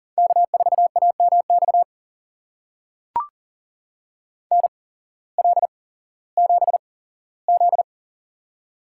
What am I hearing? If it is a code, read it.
Morse code, heard as a keyed tone, about 40 wpm, spaced as for 19 wpm, 700 Hz. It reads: K4AMX NL7Z
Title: USA Call Signs - 8x Character Spacing - 40wpm